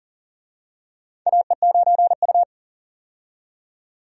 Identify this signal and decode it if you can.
Morse code, heard as a keyed tone, about 40 wpm, 700 Hz. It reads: AE9U
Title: USA Call Signs - 4 Characters - 40wpm